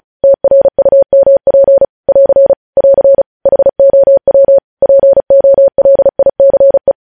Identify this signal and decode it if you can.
Morse code, heard as a keyed tone, about 36 wpm, 550 Hz. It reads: TRUMP <AR> <AR> HOW POLICE
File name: podcast / News headlines-1x-35x35